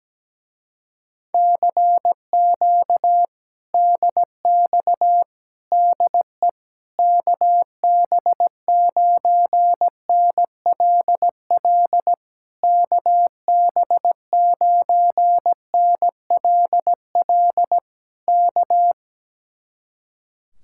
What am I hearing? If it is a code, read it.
Morse code, heard as a keyed tone, about 17 wpm, 700 Hz. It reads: CQ DX DE KB9NLL KB9NLL K
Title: Speed-Racing Common QSO Phrases - 17wpm